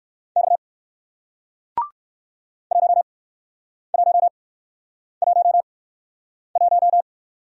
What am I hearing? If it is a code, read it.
Morse code, heard as a keyed tone, about 60 wpm, 700 Hz. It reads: X 1 1 1 1